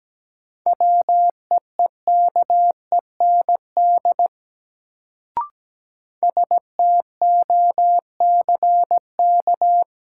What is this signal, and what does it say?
Morse code, heard as a keyed tone, about 17 wpm, 700 Hz. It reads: WEEKEND STOCK
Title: Sets of 5 Words - New Word Drill - 17wpm